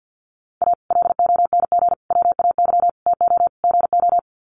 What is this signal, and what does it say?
Morse code, heard as a keyed tone, about 48 wpm, 700 Hz. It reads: A LONG WAY TO GO
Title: Sets of 5 Words 50wpm